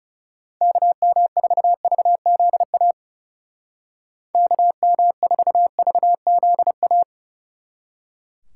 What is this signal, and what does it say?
Morse code, heard as a keyed tone, about 38 wpm, 700 Hz. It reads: KM4VZA KM4VZA